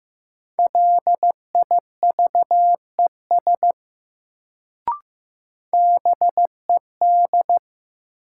Morse code, heard as a keyed tone, about 15 wpm, 700 Hz. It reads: LIVES BED
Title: Sets of 5 Words Encore - New Word Drill - 15wpm